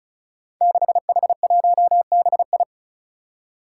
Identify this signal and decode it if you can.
Morse code, heard as a keyed tone, about 34 wpm, 700 Hz. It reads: 6H1BI
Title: Five Challenging Letter-Number Combinations 35wpm